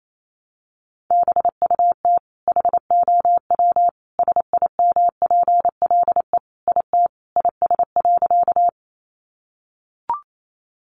Morse code, heard as a keyed tone, about 28 wpm, 700 Hz. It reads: BUT HOW SIMPLE IT IS.
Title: Sentences from Top 400 Words 28wpm